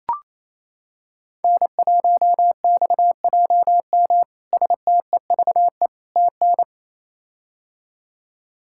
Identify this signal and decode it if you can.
Morse code, heard as a keyed tone, about 28 wpm, 700 Hz. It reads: N1XJM STEVE TN
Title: K1USN SST QSO Practice - 28wpm